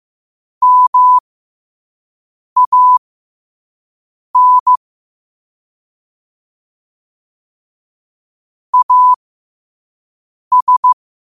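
Morse code, heard as a keyed tone, about 15 wpm, spaced as for 5 wpm, 1000 Hz. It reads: MAN AS
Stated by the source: Morse code quote of the day sent at 5 wpm.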